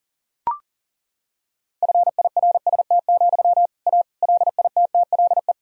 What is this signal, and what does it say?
Morse code, heard as a keyed tone, about 37 wpm, 700 Hz. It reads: FIRST, A LITTLE
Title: Sentences from Top 1000 Words - Review Part 2 - 40wpm